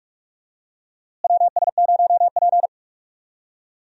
Morse code, heard as a keyed tone, about 45 wpm, 700 Hz. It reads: WS0P
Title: USA Call Signs - 4 Characters - Repeated 3x - 45wpm